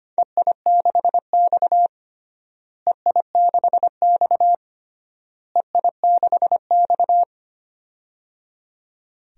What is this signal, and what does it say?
Morse code, heard as a keyed tone, about 25 wpm, 700 Hz. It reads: EI6X EI6X EI6X
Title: DXCC-100 Call Signs - 4 Characters - Repeated 3x - 25wpm